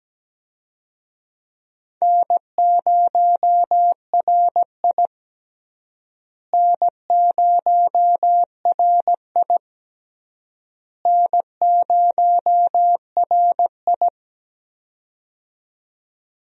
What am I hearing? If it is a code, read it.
Morse code, heard as a keyed tone, about 17 wpm, 700 Hz. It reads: N0RI N0RI N0RI
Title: USA Call Signs - 4 Characters - Repeated 3x - 17wpm